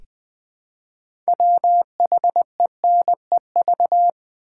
Morse code, heard as a keyed tone, about 20 wpm, 700 Hz. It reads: WHENEV
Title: Build A Word - 3 Letters At A Time - 8-Letter Words - 20wpm